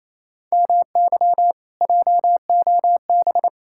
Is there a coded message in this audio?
Yes: Morse code from 0.5 to 3.5 s, heard as a keyed tone, about 37 wpm, 700 Hz. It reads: MY JOB